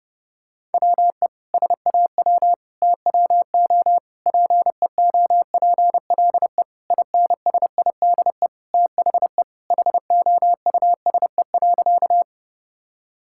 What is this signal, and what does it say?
Morse code, heard as a keyed tone, about 30 wpm, 700 Hz. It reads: WE SAW TWO PEOPLE INSIDE THE HOUSE.
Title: Sentences from Top 1000 Words - Review Part 6 - 30wpm